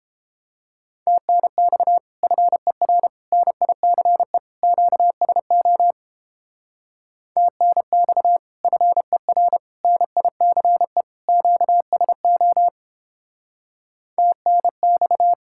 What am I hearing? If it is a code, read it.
Morse code, heard as a keyed tone, about 29 wpm, 700 Hz. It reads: TNX FER NICE QSO TNX FER NICE QSO TNX